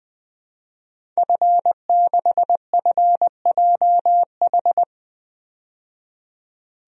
Morse code, heard as a keyed tone, about 20 wpm, 700 Hz. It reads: F6FJH
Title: DXCC-100 Call Signs - 5 Characters - 20wpm